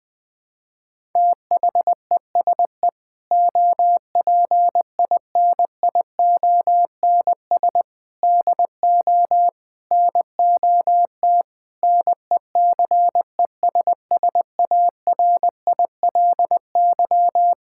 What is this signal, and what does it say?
Morse code, heard as a keyed tone, about 20 wpm, 700 Hz. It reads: THESE OPINIONS DO NOT NECESSARILY
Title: Sentences with 8-Word Phrases - 20wpm